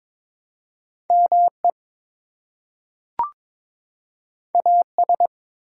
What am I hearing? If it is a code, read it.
Morse code, heard as a keyed tone, about 23 wpm, 700 Hz. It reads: ME AS
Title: Two Letter Words - 22wpm